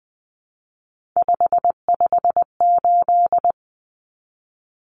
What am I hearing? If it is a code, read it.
Morse code, heard as a keyed tone, about 20 wpm, 700 Hz. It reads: 558
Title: Common Cut Numbers - 3-Digit 20wpm